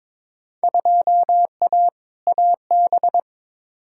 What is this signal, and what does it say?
Morse code, heard as a keyed tone, about 22 wpm, 700 Hz. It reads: 2A AB
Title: Field Day Exchanges - 22wpm